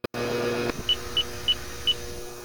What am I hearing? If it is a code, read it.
There is no code.